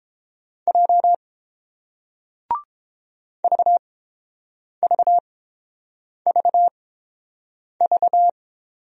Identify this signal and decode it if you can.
Morse code, heard as a keyed tone, about 30 wpm, spaced as for 7 wpm, 700 Hz. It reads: J VVVV